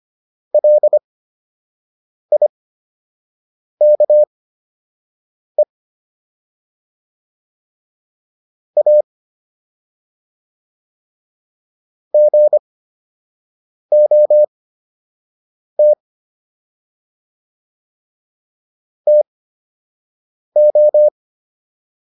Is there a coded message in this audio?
Yes: Morse code from 0.5 to 21.1 s, heard as a keyed tone, about 25 wpm, spaced as for 6 wpm, 600 Hz. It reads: LIKE A GOT TO